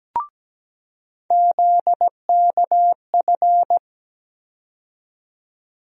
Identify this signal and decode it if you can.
Morse code, heard as a keyed tone, about 17 wpm, 700 Hz. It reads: ZKF
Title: USA Call Sign Postfixes - 3 Letters 17wpm